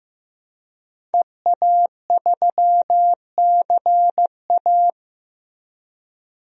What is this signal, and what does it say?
Morse code, heard as a keyed tone, about 16 wpm, 700 Hz. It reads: EA3CA
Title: DXCC-100 Call Signs - 5 Characters - Repeated 3x - 15wpm